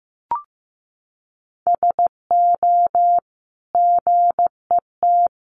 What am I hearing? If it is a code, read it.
Morse code, heard as a keyed tone, about 15 wpm, 700 Hz. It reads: SO GET
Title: Sentences from Top 1000 Words 15wpm